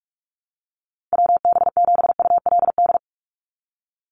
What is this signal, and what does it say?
Morse code, heard as a keyed tone, about 44 wpm, 700 Hz. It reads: WB7ULD
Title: NA SOTA Chaser Callsigns 45wpm